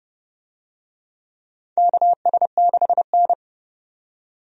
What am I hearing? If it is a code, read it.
Morse code, heard as a keyed tone, about 29 wpm, 700 Hz. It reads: KS6N
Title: USA Call Signs - 4 Characters - 30wpm